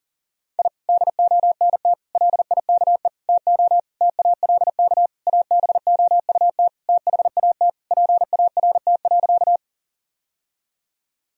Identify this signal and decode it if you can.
Morse code, heard as a keyed tone, about 37 wpm, 700 Hz. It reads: I DONT LIKE TO TALK ABOUT THAT PART.
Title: Sentences from Top 1000 Words - Review Part 1 - 40wpm